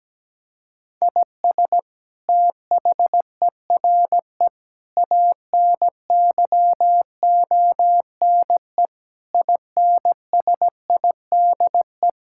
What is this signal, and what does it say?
Morse code, heard as a keyed tone, about 17 wpm, 700 Hz. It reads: IS THERE ANYONE INSIDE
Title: Sentences from Top 1000 Words - Review Part 6 - 17wpm